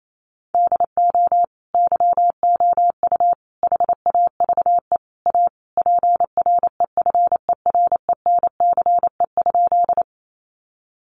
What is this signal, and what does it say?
Morse code, heard as a keyed tone, about 28 wpm, 700 Hz. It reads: DO YOU HAVE A PREFERENCE?
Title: Sentences from Top 1500 Words 28wpm